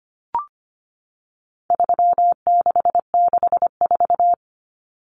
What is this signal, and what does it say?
Morse code, heard as a keyed tone, about 25 wpm, 700 Hz. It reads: 3664
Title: Common Cut Numbers - 4-Digit 25wpm